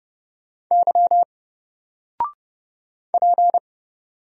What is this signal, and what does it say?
Morse code, heard as a keyed tone, about 30 wpm, 700 Hz. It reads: Y P
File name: Single-Letter-Number-30wpm